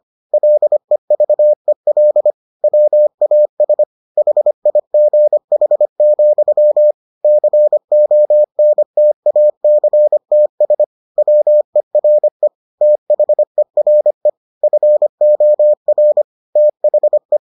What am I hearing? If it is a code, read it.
Morse code, heard as a keyed tone, about 25 wpm, 600 Hz. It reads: LEVEL WAS HIGH, CONTACTS WERE THERE FOR THE